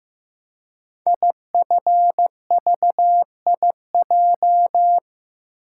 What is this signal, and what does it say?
Morse code, heard as a keyed tone, about 15 wpm, 700 Hz. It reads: IFVIJ
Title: Code Groups - 5 Letters - 15wpm